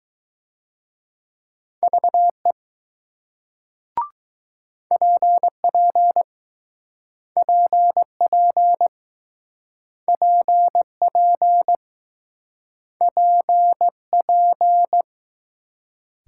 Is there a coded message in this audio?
Yes: Morse code from 1.8 to 15.0 s, heard as a keyed tone, about 20 wpm, 700 Hz. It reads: VE PP PP PP PP